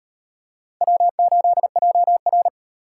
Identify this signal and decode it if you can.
Morse code, heard as a keyed tone, about 35 wpm, 700 Hz. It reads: W8JR